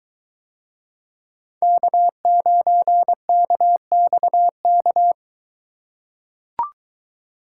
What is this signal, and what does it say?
Morse code, heard as a keyed tone, about 22 wpm, 700 Hz. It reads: K9KXK